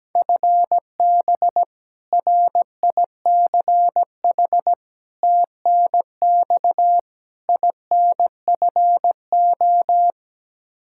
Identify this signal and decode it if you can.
Morse code, heard as a keyed tone, about 17 wpm, 700 Hz. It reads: FB RICH TNX INFO
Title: Practice QSO - Exchange 3 - More Info - 17wpm